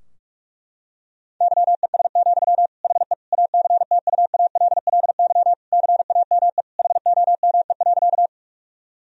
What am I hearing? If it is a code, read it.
Morse code, heard as a keyed tone, about 46 wpm, 700 Hz. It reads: YES, HE ACTUALLY CAME HOME.
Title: Sentences from Top 1000 Words - Review Part 4 - 45wpm